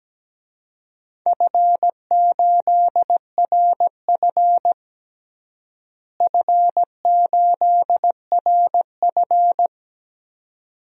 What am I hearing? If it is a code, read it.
Morse code, heard as a keyed tone, about 17 wpm, 700 Hz. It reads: F8RF F8RF
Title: DXCC-100 Call Signs - 4 Characters - Repeated 2x - 17wpm